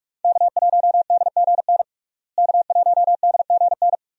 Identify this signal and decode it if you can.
Morse code, heard as a keyed tone, about 44 wpm, 700 Hz. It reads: K1DGN K1DGN
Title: USA Call Signs - 5 Characters - Repeated 2x - 45wpm